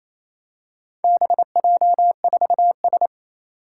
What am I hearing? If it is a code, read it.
Morse code, heard as a keyed tone, about 28 wpm, 700 Hz. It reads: BJ4S
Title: Four Challenging Letter-Number Combinations 28wpm